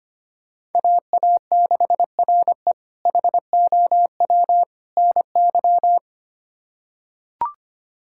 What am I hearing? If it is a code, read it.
Morse code, heard as a keyed tone, about 25 wpm, 700 Hz. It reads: AA6RE HOW NY
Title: K1USN SST QSO Practice - 25wpm